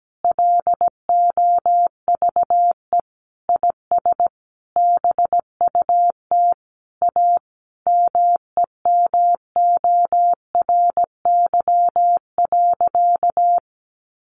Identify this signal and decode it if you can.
Morse code, heard as a keyed tone, about 17 wpm, 700 Hz. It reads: LOVE IS BUT A MEMORY.
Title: Sentences from Top 1300 Words 17wpm